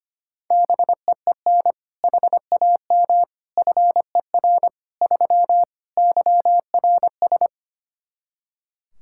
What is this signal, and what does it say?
Morse code, heard as a keyed tone, about 25 wpm, 700 Hz. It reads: BEEN HAM FER 3 YRS
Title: K1USN SST QSO Practice 25wpm